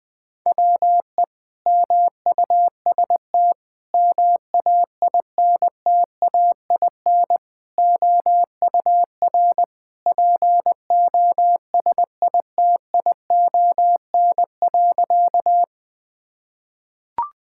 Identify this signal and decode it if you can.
Morse code, heard as a keyed tone, about 20 wpm, 700 Hz. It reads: WE MUST MAINTAIN OUR POSITION.
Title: Sentences from Top 1000 Words - Review Part 5 - 20wpm